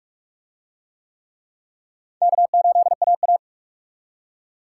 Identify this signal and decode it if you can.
Morse code, heard as a keyed tone, about 41 wpm, 700 Hz. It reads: K8AA